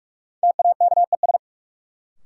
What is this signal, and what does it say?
Morse code, heard as a keyed tone, about 46 wpm, 700 Hz. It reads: TAKES
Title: Sentences from Top 1000 Words - Review Part 6 - New Word Drill - 45wpm